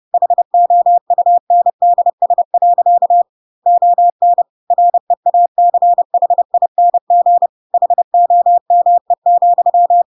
Morse code, heard as a keyed tone, about 30 wpm, 700 Hz. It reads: HOUNDS. ON REACHING HOME,